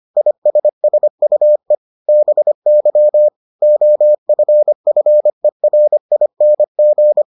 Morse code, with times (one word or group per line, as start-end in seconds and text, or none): 0.2-1.8 ISSUE
2.1-3.3 BY
3.6-7.2 OFFERING